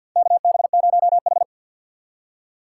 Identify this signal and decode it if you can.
Morse code, heard as a keyed tone, about 52 wpm, 700 Hz. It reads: KB0H
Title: USA Call Signs - 4 Characters - 2x Character Spacing - 50wpm